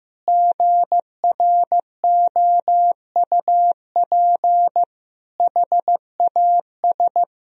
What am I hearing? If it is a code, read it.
Morse code, heard as a keyed tone, about 15 wpm, 700 Hz. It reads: GROUP HAS
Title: Sentences from Top 1000 Words - Review Part 1 - 15wpm